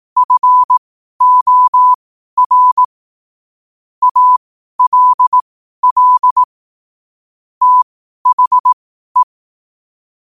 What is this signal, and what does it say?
Morse code, heard as a keyed tone, about 18 wpm, spaced as for 12 wpm, 1000 Hz. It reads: FOR ALL THE
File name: qotdfiles / 12wpm